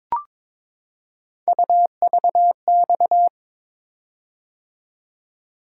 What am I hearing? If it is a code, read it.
Morse code, heard as a keyed tone, about 23 wpm, 700 Hz. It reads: UVX
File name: practice / 22wpm / Three Letter-Number Combinations - 22wpm